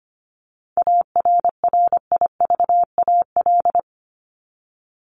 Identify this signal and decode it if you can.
Morse code, heard as a keyed tone, about 25 wpm, 700 Hz. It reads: ARRIVAL
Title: Top 1751-2000 Words 25wpm